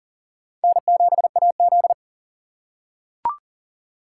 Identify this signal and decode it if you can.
Morse code, heard as a keyed tone, about 40 wpm, 700 Hz. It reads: N7AZ
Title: K1USN SST QSO Practice - 40wpm